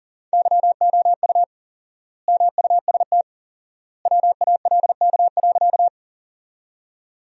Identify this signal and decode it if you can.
Morse code, heard as a keyed tone, about 40 wpm, 700 Hz. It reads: YOU MUST WALK.
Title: 4x Word-Spacing - Sentences from Top 300 Words - 40wpm